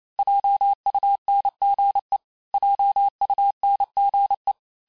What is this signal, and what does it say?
Morse code, heard as a keyed tone, about 28 wpm, 800 Hz. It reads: JUNGE JUNGE
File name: cours-du-mercredi / audios / cours-865-teste-les-mots-vitesse-27mpm